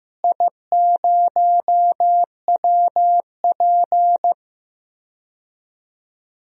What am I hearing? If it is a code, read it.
Morse code, heard as a keyed tone, about 15 wpm, 700 Hz. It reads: I0WP
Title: DXCC-100 Call Signs - 4 Characters - Repeated 3x - 15wpm